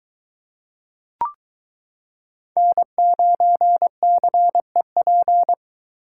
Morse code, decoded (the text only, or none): N9CEP